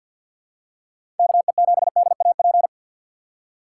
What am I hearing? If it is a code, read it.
Morse code, heard as a keyed tone, about 47 wpm, 700 Hz. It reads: KE7DAP
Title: USA Call Signs - 6 Characters - 4x Character Spacing - 50wpm